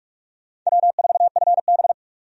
Morse code, heard as a keyed tone, about 46 wpm, 700 Hz. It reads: W4FB
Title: USA Call Signs - 4 Characters - 45wpm